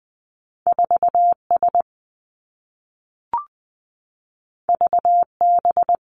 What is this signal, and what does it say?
Morse code, heard as a keyed tone, about 20 wpm, 700 Hz. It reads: 4 S V B
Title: Two Challenging Letter-Number Combinations 20wpm